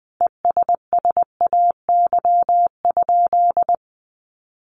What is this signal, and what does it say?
Morse code, heard as a keyed tone, about 20 wpm, 700 Hz. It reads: ESSAY?
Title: Sentences from Top 1500 Words 20wpm